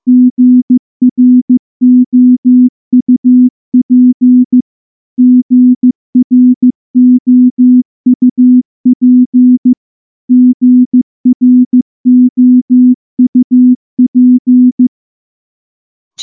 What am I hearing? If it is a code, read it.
Morse code, heard as a keyed tone, about 14 wpm, 250 Hz. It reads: GROUP GROUP GROUP
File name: morse-training-15WPM